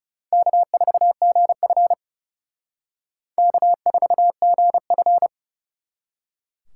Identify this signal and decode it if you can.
Morse code, heard as a keyed tone, about 37 wpm, 700 Hz. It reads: K4GF K4GF
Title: Speed Racing - USA 1x2 Call Signs 30wpm